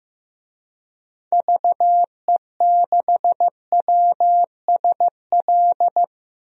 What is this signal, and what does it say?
Morse code, heard as a keyed tone, about 15 wpm, 700 Hz. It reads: VE6WSL
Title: DXCC-100 Call Signs - 6 Characters - Repeated 2x - 15wpm